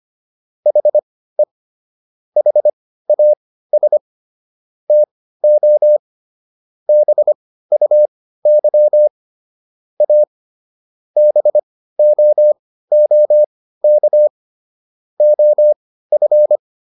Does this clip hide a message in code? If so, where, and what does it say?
Yes: Morse code from 0.7 to 16.6 s, heard as a keyed tone, about 25 wpm, spaced as for 15 wpm, 600 Hz. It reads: HE HAS TO BUY A BOOK OF